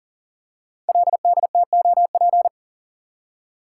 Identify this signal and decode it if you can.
Morse code, heard as a keyed tone, about 40 wpm, 700 Hz. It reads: LDTOP
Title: Code Groups - 5 Letters - 40wpm